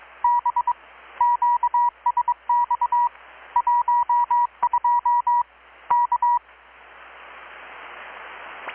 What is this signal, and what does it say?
Morse code, heard as a keyed tone, about 22 wpm, 960 Hz. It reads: B QSX 12 K
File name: blog / swl / 6-abril-2019-12695-USB-KPS